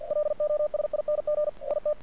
Morse code, heard as a keyed tone, about 50 wpm, 600 Hz. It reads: LOSING IT
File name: archived / CRSnet